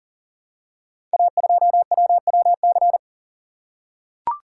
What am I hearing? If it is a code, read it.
Morse code, heard as a keyed tone, about 40 wpm, 690 Hz. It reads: A2WWC
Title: Code Groups - 5 Letter:Numbers - 40wpm